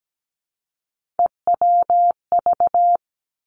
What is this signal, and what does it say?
Morse code, heard as a keyed tone, about 17 wpm, 700 Hz. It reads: EWV
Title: Three Challenging Letter-Number Combinations 17wpm